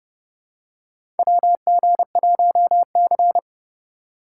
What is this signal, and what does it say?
Morse code, heard as a keyed tone, about 30 wpm, 700 Hz. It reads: WG1C
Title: USA Call Signs - 4 Characters - 2x Character Spacing - 30wpm